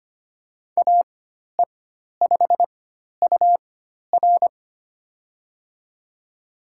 Morse code, heard as a keyed tone, about 25 wpm, spaced as for 12 wpm, 700 Hz. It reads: AE5UR
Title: USA Call Signs - 5 Characters - 4x Character Spacing - 25wpm